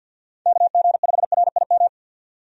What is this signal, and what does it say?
Morse code, heard as a keyed tone, about 52 wpm, 700 Hz. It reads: KG5RIM
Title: USA Call Signs - 6 Characters - 50wpm